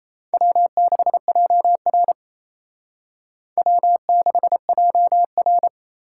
Morse code, heard as a keyed tone, about 34 wpm, 700 Hz. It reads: W6JR W6JR
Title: Speed Racing - USA 1x2 Call Signs 28wpm